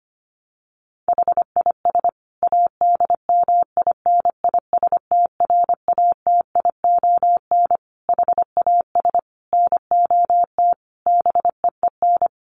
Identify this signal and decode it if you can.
Morse code, heard as a keyed tone, about 25 wpm, 700 Hz. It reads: HIS ADMINISTRATION HAS NOT BEEN
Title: Sentences from Top 1300 Words 25wpm